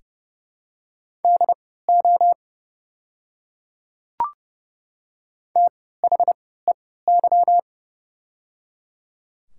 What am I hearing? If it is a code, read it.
Morse code, heard as a keyed tone, about 30 wpm, spaced as for 17 wpm, 700 Hz. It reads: DO THEY
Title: 3x Character Spacing - Top 100 Words - 30wpm